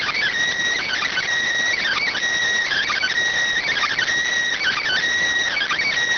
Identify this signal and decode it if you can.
Emergency Alert System (SAME) tones, heard as harsh buzzing bursts. It is heard from 0.8 to 5.0 s.